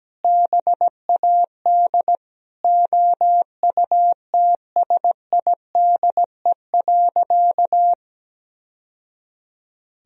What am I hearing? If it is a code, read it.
Morse code, heard as a keyed tone, about 17 wpm, 700 Hz. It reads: BAD OUTSIDE.
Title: Sentences from Top 1000 Words - Review Part 4 - 17wpm